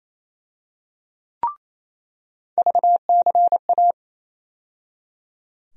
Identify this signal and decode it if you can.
Morse code, heard as a keyed tone, about 27 wpm, 700 Hz. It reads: VCA
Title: USA Call Sign Postfixes - 3 Letters 28wpm